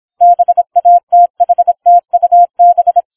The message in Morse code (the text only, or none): BATHTUB